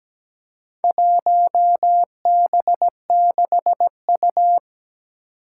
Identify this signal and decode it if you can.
Morse code, heard as a keyed tone, about 17 wpm, 700 Hz. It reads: 1B6U
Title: Four Challenging Letter-Number Combinations 17wpm